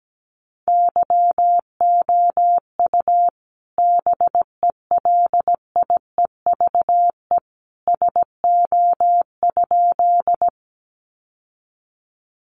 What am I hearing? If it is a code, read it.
Morse code, heard as a keyed tone, about 17 wpm, 700 Hz. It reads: YOU BELIEVE SO?
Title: Sentences from Top 1000 Words 17wpm